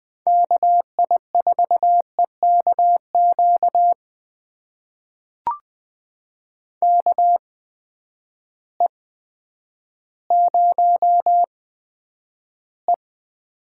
Morse code, heard as a keyed tone, about 20 wpm, spaced as for 10 wpm, 700 Hz. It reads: KI4EKQ KE0E
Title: USA Call Signs - 6 Characters - 8x Character Spacing - 20wpm